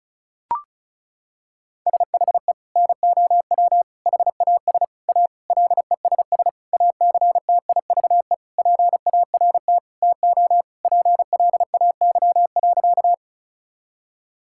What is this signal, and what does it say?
Morse code, heard as a keyed tone, about 36 wpm, 700 Hz. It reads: SHE NOW HAS A LESS ACTIVE PART TO PLAY.
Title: Sentences from Top 1000 Words - Review Part 6 - 35wpm